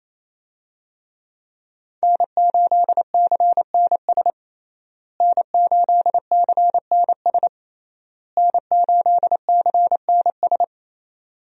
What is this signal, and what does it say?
Morse code, heard as a keyed tone, about 27 wpm, 700 Hz. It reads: N8CNS N8CNS N8CNS
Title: USA Call Signs - 5 Characters - Repeated 3x - 28wpm